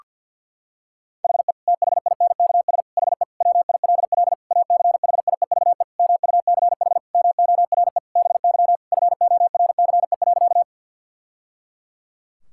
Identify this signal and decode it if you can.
Morse code, heard as a keyed tone, about 50 wpm, 700 Hz. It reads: HE THINKS HE WILL ACHIEVE MUCH MORE BY FORCE.
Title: Sentences from Top 1000 Words - Review Part 6 - 50wpm